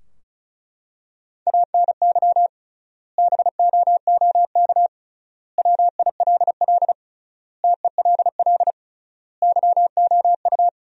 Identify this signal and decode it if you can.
Morse code, heard as a keyed tone, about 34 wpm, 700 Hz. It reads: ANY BOOK WILL TELL YOU
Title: 3x Word-Spacing - Sentences from Top 300 Words - 35wpm